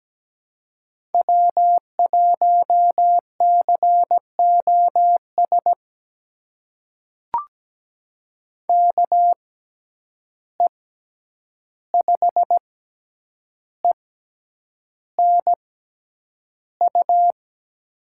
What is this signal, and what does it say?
Morse code, heard as a keyed tone, about 17 wpm, spaced as for 8 wpm, 700 Hz. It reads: W1COS KE5ENU
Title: USA Call Signs - 6x Character Spacing - 17wpm